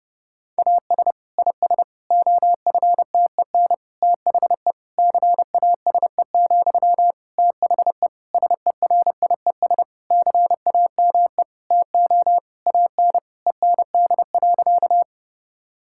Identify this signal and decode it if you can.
Morse code, heard as a keyed tone, about 30 wpm, 700 Hz. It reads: AS IS OFTEN THE CASE, THE SERIES CAME TO AN END.
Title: Sentences from Top 1000 Words - Review Part 5 - 30wpm